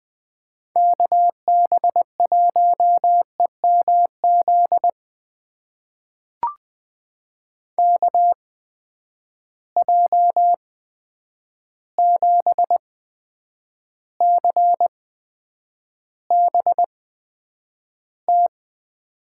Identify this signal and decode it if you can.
Morse code, heard as a keyed tone, about 20 wpm, spaced as for 9 wpm, 700 Hz. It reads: KB1EMZ KJ7CBT
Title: USA Call Signs - 6 Characters - 8x Character Spacing - 20wpm